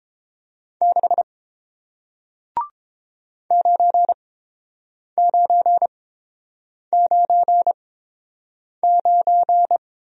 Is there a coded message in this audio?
Yes: Morse code from 0.8 to 9.8 s, heard as a keyed tone, about 30 wpm, spaced as for 7 wpm, 700 Hz. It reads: 6 9999